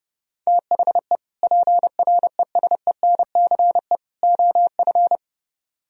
Morse code, heard as a keyed tone, about 28 wpm, 700 Hz. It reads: THE PRESENCE OF